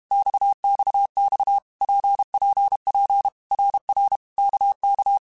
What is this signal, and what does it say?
Morse code, heard as a keyed tone, about 32 wpm, 800 Hz. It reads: XXX PPP RR KK
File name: Basic Letter Groups By7Groups / RKPX / RKPX@30